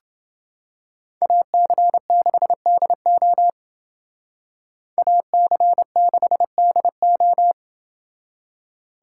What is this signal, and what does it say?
Morse code, heard as a keyed tone, about 29 wpm, 700 Hz. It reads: AC6DO AC6DO